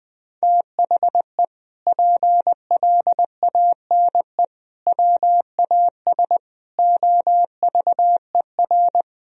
Morse code, heard as a keyed tone, about 20 wpm, 700 Hz. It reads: THE PLANE WAS OVER
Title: Sentences from Top 1000 Words - Review Part 3 - 20wpm